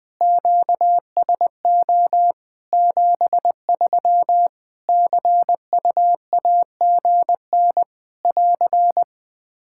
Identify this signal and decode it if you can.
Morse code, heard as a keyed tone, about 21 wpm, 700 Hz. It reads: QSO 73 CUAGN <AR>
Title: Practice QSO - Exchange 4 - Ending - 20wpm